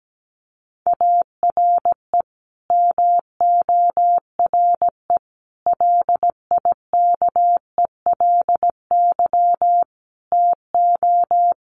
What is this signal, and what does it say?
Morse code, heard as a keyed tone, about 17 wpm, 700 Hz. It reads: ARE MORE LIKELY TO
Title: Sets of 5 Words 17wpm